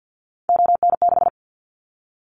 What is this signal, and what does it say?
Morse code, heard as a keyed tone, about 50 wpm, 700 Hz. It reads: ON6
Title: Callsign Prefixes for Top 50 DXCC - 50wpm